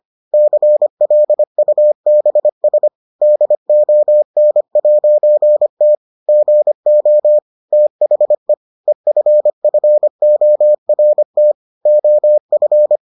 Morse code, heard as a keyed tone, about 25 wpm, 600 Hz. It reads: CLUBS DON'T GO THE EFFORT OF